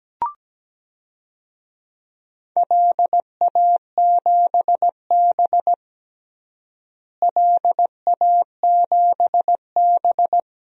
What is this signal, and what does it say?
Morse code, heard as a keyed tone, about 17 wpm, 700 Hz. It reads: LA7B LA7B
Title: DXCC-100 Call Signs - 4 Characters - Repeated 2x - 17wpm